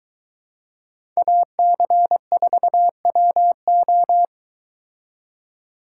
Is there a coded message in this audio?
Yes: Morse code from 1.2 to 4.2 s, heard as a keyed tone, about 23 wpm, 700 Hz. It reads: AC4WO